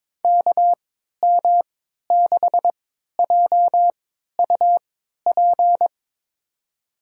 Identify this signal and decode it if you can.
Morse code, heard as a keyed tone, about 22 wpm, spaced as for 13 wpm, 700 Hz. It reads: KM6JUP
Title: USA Call Signs - 6 Characters - 3x Character Spacing - 22wpm